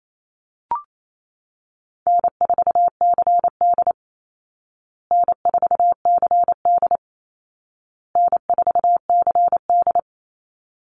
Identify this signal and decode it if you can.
Morse code, heard as a keyed tone, about 28 wpm, 700 Hz. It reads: N4CD N4CD N4CD
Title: Call Signs Repeated 2x - 28wpm